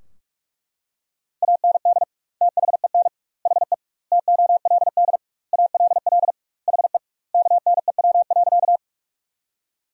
Morse code, heard as a keyed tone, about 41 wpm, 700 Hz. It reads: AND THEN HE TOLD ALL HE KNEW.
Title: 2x Word-Spacing - Sentences from Top 400 Words - 45wpm